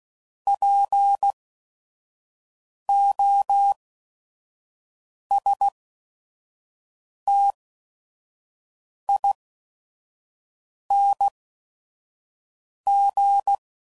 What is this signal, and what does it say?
Morse code, heard as a keyed tone, about 16 wpm, spaced as for 5 wpm, 800 Hz. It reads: POSTING
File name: Suffixes / Special ING's / VerySlowToFaster ING's